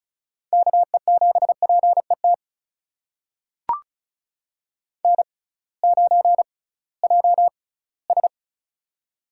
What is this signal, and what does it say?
Morse code, heard as a keyed tone, about 34 wpm, spaced as for 20 wpm, 700 Hz. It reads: KE7PET N9JS
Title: USA Call Signs - 6x Character Spacing - 35wpm